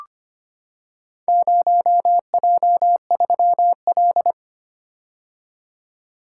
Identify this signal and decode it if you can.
Morse code, heard as a keyed tone, about 25 wpm, 700 Hz. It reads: 0J3L
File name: practice / 25wpm / Four Letter-Number Combinations - 25wpm